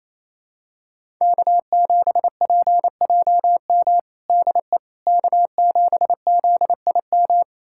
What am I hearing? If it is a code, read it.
Morse code, heard as a keyed tone, about 28 wpm, 700 Hz. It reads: K7PJM DE K7ZIM
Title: USA Callsign DE Callsign - 5 Characters - 28wpm